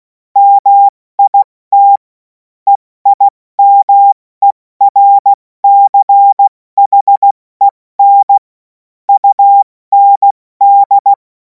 Morse code, heard as a keyed tone, about 16 wpm, spaced as for 14 wpm, 800 Hz. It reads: MIT EIMERCHEN UND
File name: Morsekurs / Witze / Witz 03. März 2026 70 F80